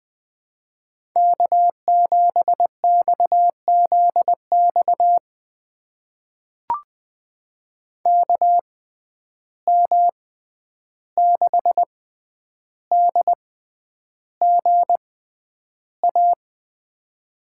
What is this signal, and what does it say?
Morse code, heard as a keyed tone, about 20 wpm, spaced as for 10 wpm, 700 Hz. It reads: K7XZX KM6DGA
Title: USA Call Signs - 6x Character Spacing - 20wpm